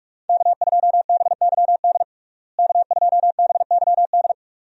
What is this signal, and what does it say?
Morse code, heard as a keyed tone, about 46 wpm, 700 Hz. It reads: K2BYD K2BYD
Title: USA Call Signs - 5 Characters - Repeated 2x - 45wpm